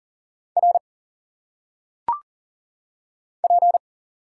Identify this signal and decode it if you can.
Morse code, heard as a keyed tone, about 40 wpm, 700 Hz. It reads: R P
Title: Single Letters - Phonetic Pronunciation 40wpm